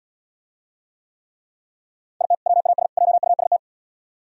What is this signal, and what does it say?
Morse code, heard as a keyed tone, about 77 wpm, 700 Hz. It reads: IT WAS JUST